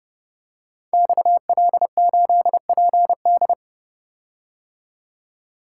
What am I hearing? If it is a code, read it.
Morse code, heard as a keyed tone, about 30 wpm, 700 Hz. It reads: XL8PD
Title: Code Groups - 5 Letter:Numbers - 30wpm